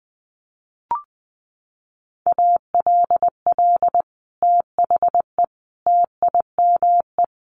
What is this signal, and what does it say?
Morse code, heard as a keyed tone, about 20 wpm, 700 Hz. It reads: ALL THE TIME
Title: Sets of 3 Words 20wpm